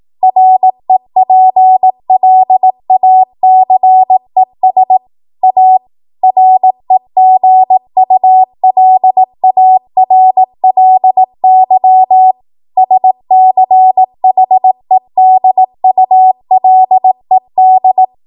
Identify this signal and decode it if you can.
Morse code, heard as a keyed tone, about 18 wpm, 750 Hz. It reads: REPLACES A REGULARLY SCHEDULED
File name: Archive / 18 WPM / 18WPM